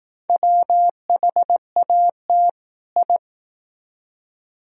Morse code, heard as a keyed tone, about 18 wpm, 700 Hz. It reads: WHAT I
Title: Speed-Racing - Sets of 2 Words 15wpm